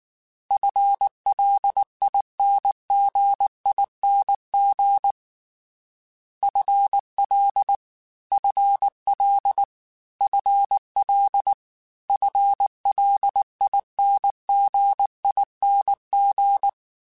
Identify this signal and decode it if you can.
Morse code, heard as a keyed tone, about 19 wpm, 800 Hz. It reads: FLINGING FL FL FL FLINGING
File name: FL / FL mp3s / FL Prefixes 3x2 Twice @18